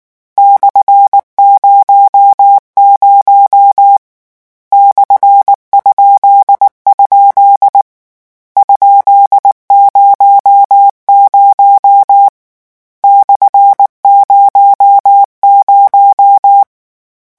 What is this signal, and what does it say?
Morse code, heard as a keyed tone, about 19 wpm, 800 Hz. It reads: /00 /?? ?00 /00
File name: morse mp3 / lecon13-vitesse 18